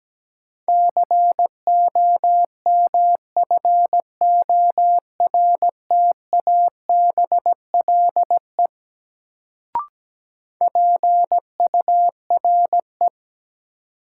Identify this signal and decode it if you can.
Morse code, heard as a keyed tone, about 17 wpm, 700 Hz. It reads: COMFORTABLE PURE
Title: Sentences from Top 1500 Words - New Word Drill Alt - 17wpm